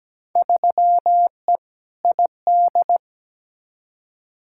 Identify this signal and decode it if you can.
Morse code, heard as a keyed tone, about 17 wpm, 700 Hz. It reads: 3E ID
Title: Field Day Exchanges Encore - 17wpm